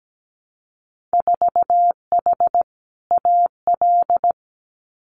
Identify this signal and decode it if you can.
Morse code, heard as a keyed tone, about 17 wpm, 700 Hz. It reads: 4H AL
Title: Winter Field Day Exchanges Encore 17wpm